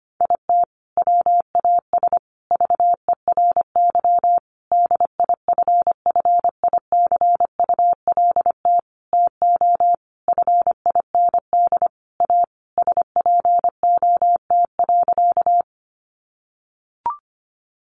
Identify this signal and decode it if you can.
Morse code, heard as a keyed tone, about 25 wpm, 700 Hz. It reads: IT WAS VERY DIFFICULT TO FIND A SPOT.
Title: Sentences from Top 1100 Words 25wpm